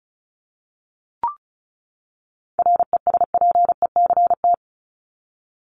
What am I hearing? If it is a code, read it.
Morse code, heard as a keyed tone, about 34 wpm, 700 Hz. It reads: RESPECT
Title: Top 1501-1750 Words 35wpm